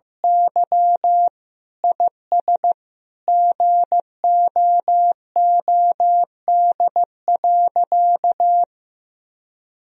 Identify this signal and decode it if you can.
Morse code, heard as a keyed tone, about 16 wpm, 700 Hz. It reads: Y IS GOOD.
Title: Sentences from Top 1000 Words - Review Part 3 - 15wpm